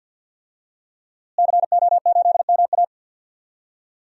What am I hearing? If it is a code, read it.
Morse code, heard as a keyed tone, about 53 wpm, 700 Hz. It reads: CO8MA
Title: DXCC-100 Call Signs - 5 Characters - Repeated 2x - 50wpm